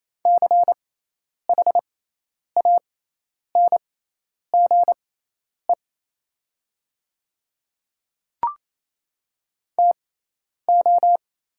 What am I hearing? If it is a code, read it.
Morse code, heard as a keyed tone, about 28 wpm, spaced as for 10 wpm, 690 Hz. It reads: CHANGE TO
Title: 6x Character Spacing - Top 300 Words - 28wpm